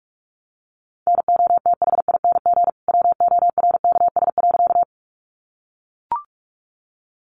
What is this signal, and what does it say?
Morse code, heard as a keyed tone, about 44 wpm, 700 Hz. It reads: NOTHING WORKS.
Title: Additional Easy Phrases 45wpm